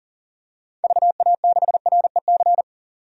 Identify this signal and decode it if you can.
Morse code, heard as a keyed tone, about 40 wpm, 700 Hz. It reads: VA6REC
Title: DXCC-100 Call Signs - 6 Characters - 40wpm